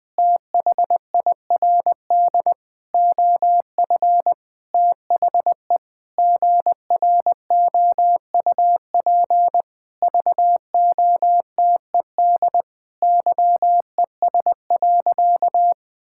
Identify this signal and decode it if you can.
Morse code, heard as a keyed tone, about 20 wpm, 700 Hz. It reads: THIRD OF THE GROUP VOTED YES.
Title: Sentences with 6-Word Phrases - 20wpm